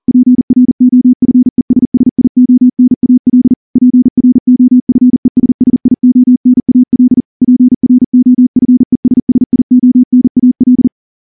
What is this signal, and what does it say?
Morse code, heard as a keyed tone, about 38 wpm, 250 Hz. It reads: PROFESSIONAL PROFESSIONAL PROFESSIONAL
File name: morse-training-40WPM